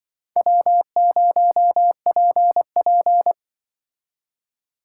Morse code, heard as a keyed tone, about 24 wpm, 700 Hz. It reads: W0PP
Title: Speed Racing - USA 1x2 Call Signs 20wpm